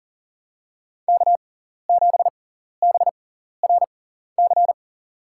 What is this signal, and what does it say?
Morse code, heard as a keyed tone, about 42 wpm, spaced as for 14 wpm, 700 Hz. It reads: K7BRC
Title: USA Call Signs - 5 Characters - 6x Character Spacing - 40wpm